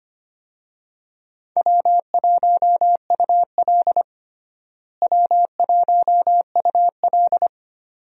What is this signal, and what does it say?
Morse code, heard as a keyed tone, about 25 wpm, 700 Hz. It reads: W1UL W1UL
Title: USA Call Signs - 4 Characters - Repeated 2x - 25wpm